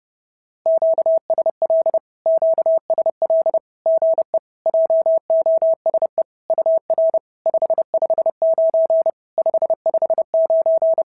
Morse code, heard as a keyed tone, about 30 wpm, 650 Hz. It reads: QSL QSL GE JOSE UR 559 559
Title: SOTA QSO Practice 30wpm